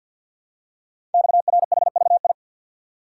Morse code, heard as a keyed tone, about 50 wpm, 690 Hz. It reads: XRHVI
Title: Code Groups - 5 Letters - 50wpm